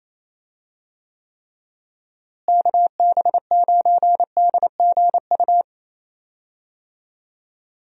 Morse code, heard as a keyed tone, about 28 wpm, 700 Hz. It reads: KB9DGU
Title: USA Call Signs - 6 Characters - Repeated 3x - 28wpm